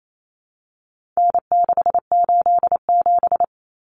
Morse code, heard as a keyed tone, about 28 wpm, 700 Hz. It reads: N687
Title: Common Cut Numbers - 4-Digit 28wpm